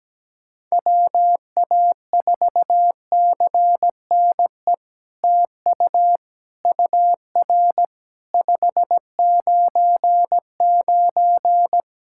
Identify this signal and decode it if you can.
Morse code, heard as a keyed tone, about 17 wpm, 700 Hz. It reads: WA4CNE TU UR 599
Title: SOTA QSO Practice 17wpm